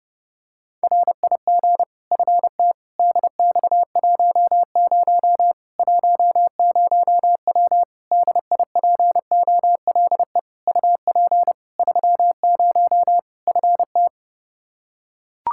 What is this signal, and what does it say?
Morse code, heard as a keyed tone, about 30 wpm, 700 Hz. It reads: RIG FT DX10 10W DIPOLE UP 30 FT
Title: Practice QSO - Exchange 2 - Info - 30wpm